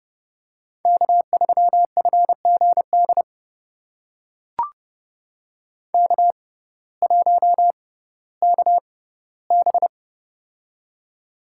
Morse code, heard as a keyed tone, about 31 wpm, spaced as for 17 wpm, 700 Hz. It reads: K3FGD K1KB
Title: USA Call Signs - 6x Character Spacing - 30wpm